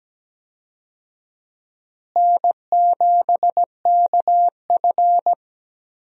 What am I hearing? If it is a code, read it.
Morse code, heard as a keyed tone, about 17 wpm, 700 Hz. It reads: N7KF
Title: USA Call Signs - 4 Characters - Repeated 3x - 17wpm